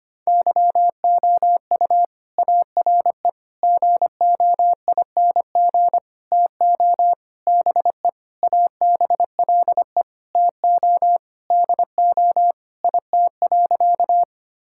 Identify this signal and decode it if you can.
Morse code, heard as a keyed tone, about 25 wpm, 700 Hz. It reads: YOU ARE GOING TO BE ABLE TO DO IT.
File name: practice / 25wpm / Sentences with 4-Word Phrases - 25wpm